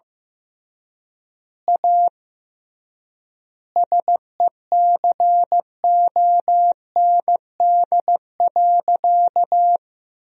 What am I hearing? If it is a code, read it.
Morse code, heard as a keyed tone, about 15 wpm, 700 Hz. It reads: A SECOND.
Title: 3x Word-Spacing - Sentences from Top 500 Words - 15wpm